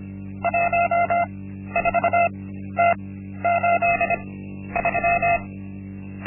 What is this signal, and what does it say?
Morse code, heard as a keyed tone, about 25 wpm, spaced as for 12 wpm, 690 Hz. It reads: 14T83
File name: Linksmakalnis-1